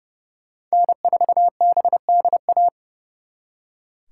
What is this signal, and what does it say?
Morse code, heard as a keyed tone, about 38 wpm, 700 Hz. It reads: N4BDA